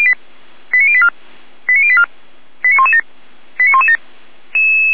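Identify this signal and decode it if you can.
Five-tone selective call (ZVEI), heard as quick runs of five tones, tones of about 70 ms. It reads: 89084 89084 89117 89117
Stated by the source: ZVEI five-tone selective call received on an SDR.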